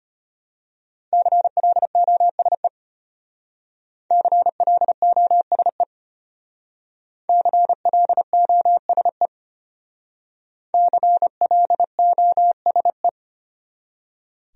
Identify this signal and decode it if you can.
Morse code, heard as a keyed tone, about 33 wpm, 700 Hz. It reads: CLOSE CLOSE CLOSE CLOSE